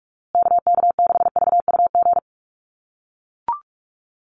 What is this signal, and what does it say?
Morse code, heard as a keyed tone, about 44 wpm, 700 Hz. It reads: KK6VUG
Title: USA Call Signs Repeated 3x 45wpm